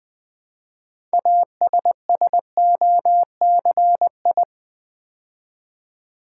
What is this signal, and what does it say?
Morse code, heard as a keyed tone, about 20 wpm, 700 Hz. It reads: ASSOCI
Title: Build A Word - 3 Letters At A Time - 9-Letter Words - 20wpm